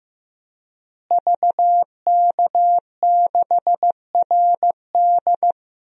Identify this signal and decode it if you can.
Morse code, heard as a keyed tone, about 15 wpm, 700 Hz. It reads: VK6RD
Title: DXCC-100 Call Signs - 5 Characters - Repeated 3x - 15wpm